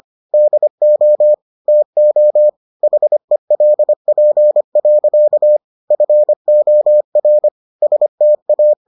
Morse code, heard as a keyed tone, about 25 wpm, 600 Hz. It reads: DO TO HELP. FOR STA